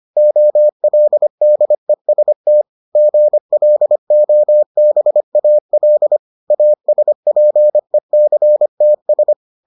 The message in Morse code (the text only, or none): OLDEST GLOBAL ASPECTS